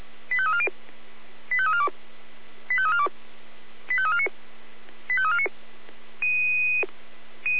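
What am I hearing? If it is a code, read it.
Five-tone selective call (ZVEI), heard as quick runs of five tones, tones of about 70 ms. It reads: 85340 85342 85342 85349 85349
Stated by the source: ZVEI received on SDR, from the audio server of the Feuerwehr Orlamünde (fire brigade).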